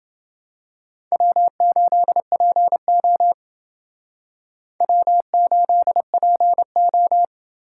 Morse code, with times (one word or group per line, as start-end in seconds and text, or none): 1.1-3.3 W8PO
4.8-7.2 W8PO